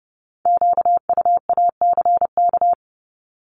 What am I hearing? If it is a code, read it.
Morse code, heard as a keyed tone, about 30 wpm, 700 Hz. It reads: QUACK
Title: Multi-letter Phonemes 30wpm